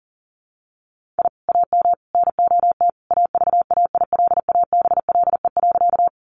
Morse code, heard as a keyed tone, about 40 wpm, 700 Hz. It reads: I AM NOT AVAILABLE.
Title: Sentences from Top 600 Words 40wpm